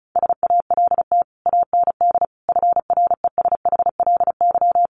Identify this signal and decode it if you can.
Morse code, heard as a keyed tone, about 34 wpm, 700 Hz. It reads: SALT AND FRESHLY
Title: Sets of 5 Words Encore 35wpm